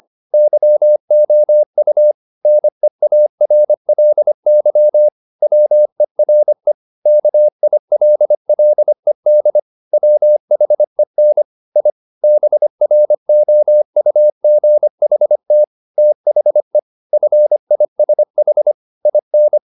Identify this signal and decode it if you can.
Morse code, heard as a keyed tone, about 25 wpm, 600 Hz. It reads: YOU NEARLY WERE KILLED WHEN I BROUGHT THE FISH IN
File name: the old man and the sea part 1 25wpm